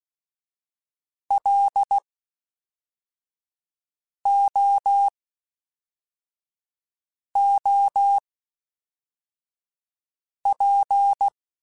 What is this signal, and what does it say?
Morse code, heard as a keyed tone, about 16 wpm, spaced as for 4 wpm, 800 Hz. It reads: LOOP
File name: (spaced) / FromSMs spaced mp3s / Classes@5